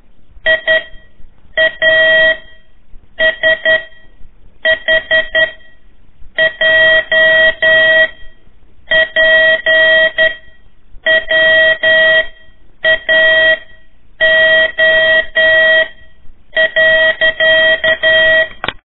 Morse code, heard as a keyed tone, about 11 wpm, 1900 Hz, with noise. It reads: I A S H J P W A O .
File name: 3 lezione CW 1 AIR